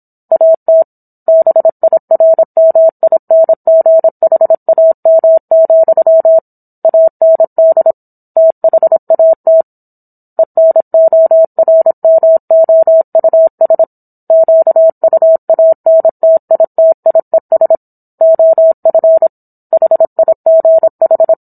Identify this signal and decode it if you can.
Morse code, heard as a keyed tone, about 27 wpm, 650 Hz. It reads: AT BIRMINGHAM, AND THAT ENORMOUS QUANTITIES OF HIGH
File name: Book One / 17-Chapter 17 - 26 WPM